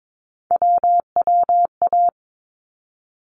Morse code, heard as a keyed tone, about 22 wpm, 700 Hz. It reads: WWA
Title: ARRL Sections 22wpm